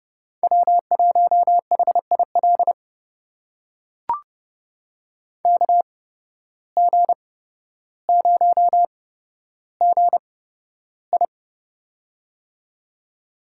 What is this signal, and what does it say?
Morse code, heard as a keyed tone, about 30 wpm, spaced as for 13 wpm, 700 Hz. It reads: W1HIL KG0GI
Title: USA Call Signs - 5 Characters - 8x Character Spacing - 30wpm